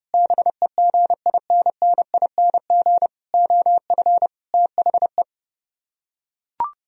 Morse code, heard as a keyed tone, about 30 wpm, 700 Hz. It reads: BEGINNING OF THE